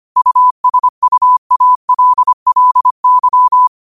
Morse code, heard as a keyed tone, about 25 wpm, 1000 Hz. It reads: USUALLY